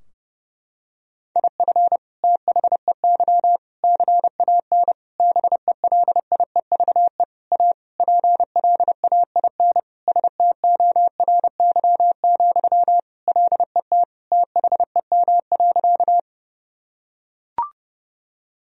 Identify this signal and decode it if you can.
Morse code, heard as a keyed tone, about 31 wpm, 700 Hz. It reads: IF THEY CAN BELIEVE A PLAIN STORY, LET THEM.
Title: Sentences from Top 1000 Words - Review Part 2 - 30wpm